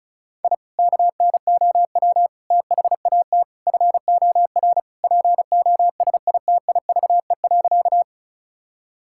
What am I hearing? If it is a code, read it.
Morse code, heard as a keyed tone, about 36 wpm, 700 Hz. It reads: I KNOW THAT FOR POSITIVE.
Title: Sentences from Top 1000 Words - Review Part 5 - 35wpm